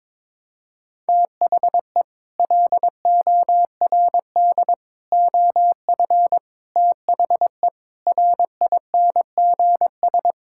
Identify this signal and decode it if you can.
Morse code, heard as a keyed tone, about 22 wpm, 700 Hz. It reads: THE LORD OF THE RINGS
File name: Sets of 9 Words - 22wpm